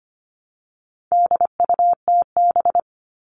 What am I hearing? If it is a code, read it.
Morse code, heard as a keyed tone, about 25 wpm, 700 Hz. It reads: DUTB
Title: All Cut Numbers - 4-Digit 25wpm